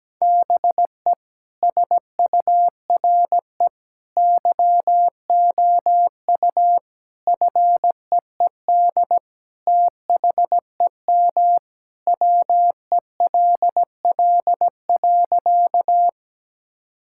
Morse code, heard as a keyed tone, about 17 wpm, 700 Hz. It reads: BE SURE YOU FEED THEM WELL.
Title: Sentences from Top 1600 Words 17wpm